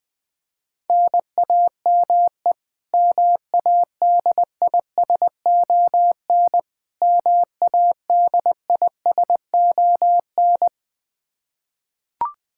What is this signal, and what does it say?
Morse code, heard as a keyed tone, about 21 wpm, 700 Hz. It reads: NAME MADISON MADISON
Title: K1USN SST QSO Practice 20wpm